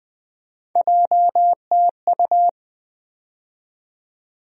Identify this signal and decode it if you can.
Morse code, heard as a keyed tone, about 21 wpm, 700 Hz. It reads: JTU
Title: USA Call Sign Postfixes - 3 Letters 20wpm